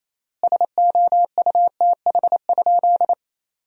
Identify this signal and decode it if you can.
Morse code, heard as a keyed tone, about 28 wpm, 700 Hz. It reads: SOUTH?
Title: 4x Word-Spacing - Sentences from Top 400 Words - 28wpm